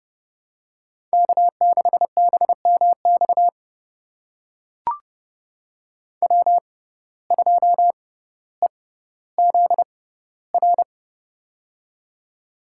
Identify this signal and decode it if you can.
Morse code, heard as a keyed tone, about 30 wpm, spaced as for 15 wpm, 700 Hz. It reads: K6BMX W2EZR
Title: USA Call Signs - 5 Characters - 6x Character Spacing - 30wpm